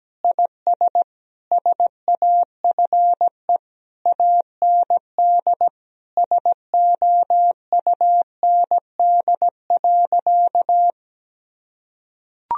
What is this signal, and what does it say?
Morse code, heard as a keyed tone, about 17 wpm, 700 Hz. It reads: IS SAFE AND SOUND.
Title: Sentences from Top 1000 Words - Review Part 1 - 17wpm